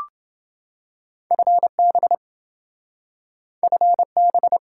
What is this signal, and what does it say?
Morse code, heard as a keyed tone, about 27 wpm, 700 Hz. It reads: FB FB